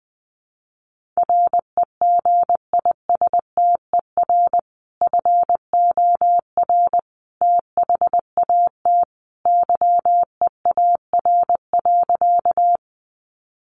Morse code, heard as a keyed tone, about 20 wpm, 700 Hz. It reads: REGISTER FOR THAT YEAR.
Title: Sentences from Top 1200 Words 20wpm